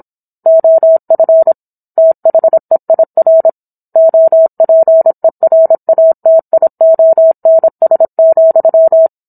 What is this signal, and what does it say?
Morse code, heard as a keyed tone, about 27 wpm, 650 Hz. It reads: OF THEIR OPERATIONS,
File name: Book One / 17-Chapter 17 - 26 WPM